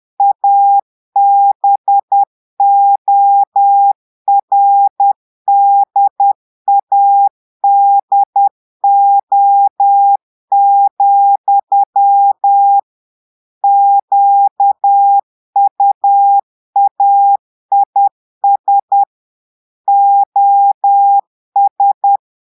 Morse code, heard as a keyed tone, about 10 wpm, 800 Hz. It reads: ABORDADO, QUAIS OS
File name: portugais texte en clair 2 vitesse 10 mots minutes